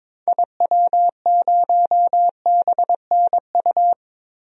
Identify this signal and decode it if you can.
Morse code, heard as a keyed tone, about 22 wpm, 700 Hz. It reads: IW0BNU
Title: DXCC-100 Call Signs - 6 Characters - 22wpm